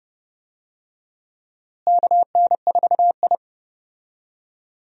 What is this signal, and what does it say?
Morse code, heard as a keyed tone, about 31 wpm, 700 Hz. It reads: KN4I
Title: USA Call Signs - 4 Characters - Repeated 3x - 30wpm